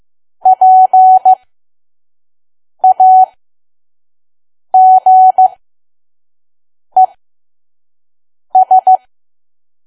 Morse code, heard as a keyed tone, about 15 wpm, spaced as for 5 wpm, 750 Hz. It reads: PAGES
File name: Archive / 5 WPM files / PM